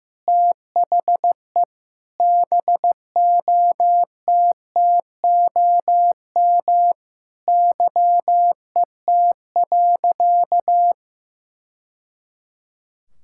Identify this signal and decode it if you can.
Morse code, heard as a keyed tone, about 15 wpm, 700 Hz. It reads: THE BOTTOM YET.
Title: Sentences from Top 1000 Words - Review Part 6 - 15wpm